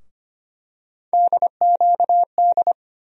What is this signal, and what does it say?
Morse code, heard as a keyed tone, about 25 wpm, 700 Hz. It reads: DQD
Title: USA Call Sign Postfixes - 3 Letters 25wpm